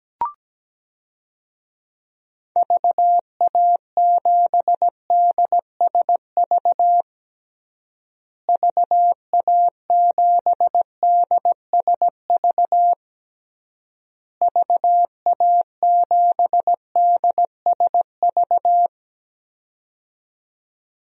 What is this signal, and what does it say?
Morse code, heard as a keyed tone, about 17 wpm, 700 Hz. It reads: VA7DSV VA7DSV VA7DSV
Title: DXCC-100 Call Signs - 6 Characters - Repeated 3x - 17wpm